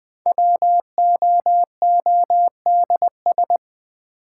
Morse code, heard as a keyed tone, about 20 wpm, 700 Hz. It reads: WOODS
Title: Sentences from Top 1000 Words - Review Part 2 - New Word Drill - 20wpm